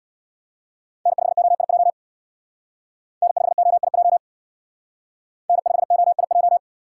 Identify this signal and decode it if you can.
Morse code, heard as a keyed tone, about 60 wpm, 700 Hz. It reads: N5OIJ N5OIJ N5OIJ
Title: Speed Racing - USA 1x3 Call Signs 50wpm